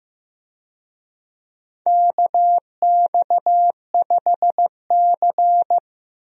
Morse code, heard as a keyed tone, about 15 wpm, 700 Hz. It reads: KX5C
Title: USA Call Signs - 4 Characters - Repeated 2x - 15wpm